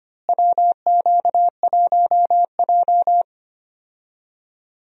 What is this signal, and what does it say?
Morse code, heard as a keyed tone, about 25 wpm, 700 Hz. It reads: WQ1J
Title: USA Call Signs - 4 Characters - Repeated 3x - 25wpm